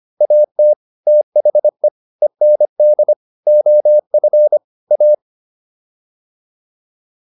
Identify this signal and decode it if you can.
Morse code, heard as a keyed tone, about 25 wpm, 600 Hz. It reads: AT THE END OF A